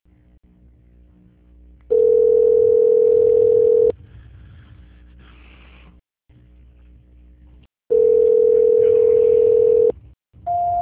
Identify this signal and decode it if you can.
Ringback tone (North America), 440 + 480 Hz.
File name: out- -2027- - - .146